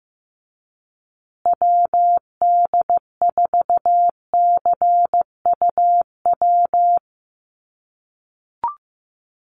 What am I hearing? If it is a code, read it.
Morse code, heard as a keyed tone, about 15 wpm, 700 Hz. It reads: WD4CUW
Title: USA Call Signs Repeated 3x 15wpm